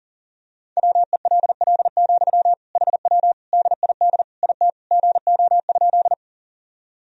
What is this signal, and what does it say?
Morse code, heard as a keyed tone, about 40 wpm, 700 Hz. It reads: WELL, HW DID IT GO?
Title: Sentences from Top 200 Words and Abbreviations 40wpm